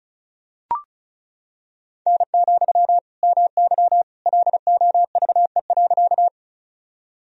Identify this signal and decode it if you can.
Morse code, heard as a keyed tone, about 36 wpm, 700 Hz. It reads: N, MY LOVE.
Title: Sentences from Top 400 Words and Abbreviations 35wpm